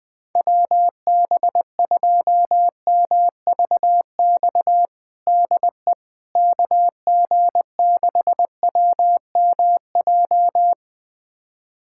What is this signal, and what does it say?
Morse code, heard as a keyed tone, about 20 wpm, 700 Hz. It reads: WB2MVX DE KG6WMJ
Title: USA Callsign DE Callsign - 6 Characters - 20wpm